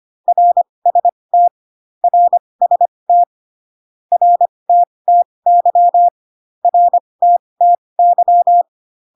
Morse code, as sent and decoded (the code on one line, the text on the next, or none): .-. ... - / .-. ... - / .-. - - -.-- / .-. - - -.--
RST RST RTTY RTTY